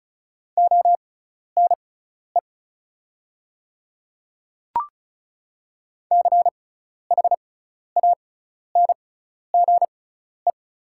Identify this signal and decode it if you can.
Morse code, heard as a keyed tone, about 36 wpm, spaced as for 12 wpm, 700 Hz. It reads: ONE CHANGE
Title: 6x Character Spacing - Top 300 Words - 35wpm